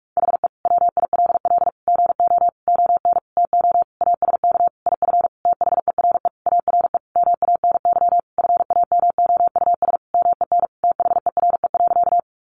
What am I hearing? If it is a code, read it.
Morse code, heard as a keyed tone, about 44 wpm, 700 Hz. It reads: HE WILL GO ON TO ASK IF THERE ARE MANY FAMOUS MEN THERE.
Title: Sentences from Top 1500 Words 45wpm